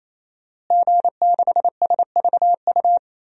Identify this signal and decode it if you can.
Morse code, heard as a keyed tone, about 28 wpm, 700 Hz. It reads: G6SVU
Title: DXCC-100 Call Signs - 5 Characters - Repeated 3x - 28wpm